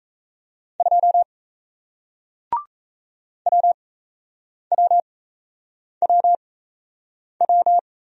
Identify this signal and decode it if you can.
Morse code, heard as a keyed tone, about 37 wpm, spaced as for 8 wpm, 700 Hz. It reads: 2 WWWW